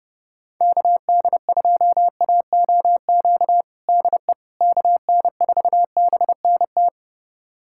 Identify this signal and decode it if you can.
Morse code, heard as a keyed tone, about 30 wpm, 700 Hz. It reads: KD2AOQ DE KN4BNT
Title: USA Callsign DE Callsign - 6 Characters - 30wpm